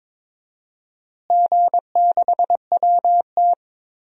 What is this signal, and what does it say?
Morse code, heard as a keyed tone, about 22 wpm, 700 Hz. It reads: G6WT
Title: DXCC-100 Call Signs - 4 Characters - Repeated 2x - 22wpm